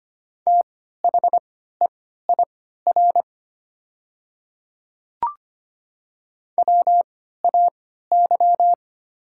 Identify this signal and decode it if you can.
Morse code, heard as a keyed tone, about 25 wpm, spaced as for 14 wpm, 690 Hz. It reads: THEIR WAY
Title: 3x Character Spacing - Top 100 Words - 25wpm